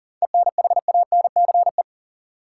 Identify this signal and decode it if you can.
Morse code, heard as a keyed tone, about 40 wpm, 700 Hz. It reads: ENHANCE
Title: Seven Letter Words - 40wpm